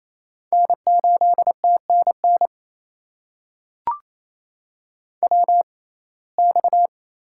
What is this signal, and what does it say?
Morse code, heard as a keyed tone, about 28 wpm, spaced as for 20 wpm, 700 Hz. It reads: N8TNN WX
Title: USA Call Signs - 6x Character Spacing - 28wpm